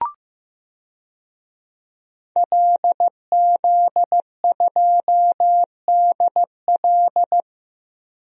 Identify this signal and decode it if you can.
Morse code, heard as a keyed tone, about 15 wpm, 700 Hz. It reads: LZ2DL
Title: DXCC-100 Call Signs - 5 Characters - Repeated 3x - 15wpm